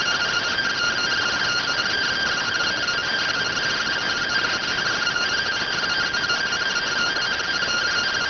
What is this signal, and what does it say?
Two-tone teleprinter signal, about 170 Hz shift.